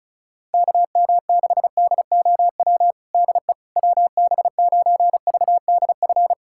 Morse code, heard as a keyed tone, about 34 wpm, 700 Hz. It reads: KM6DOW DE WB9VDF
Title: USA Callsign DE Callsign - 6 Characters - 35wpm